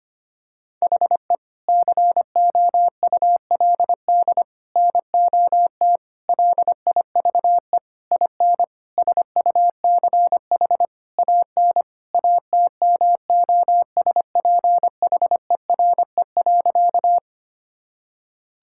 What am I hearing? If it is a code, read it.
Morse code, heard as a keyed tone, about 25 wpm, 700 Hz. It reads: HE COULD NOT LIVE IN SUCH AN ATMOSPHERE.
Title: Sentences from Top 1600 Words 25wpm